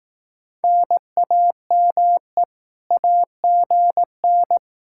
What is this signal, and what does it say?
Morse code, heard as a keyed tone, about 18 wpm, 700 Hz. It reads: NAME AGN
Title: Speed-Racing Common QSO Phrases Encore - 15wpm